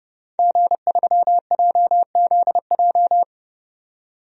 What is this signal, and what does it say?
Morse code, heard as a keyed tone, about 32 wpm, 700 Hz. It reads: G3JZJ
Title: DXCC-100 Call Signs - 5 Characters - 30wpm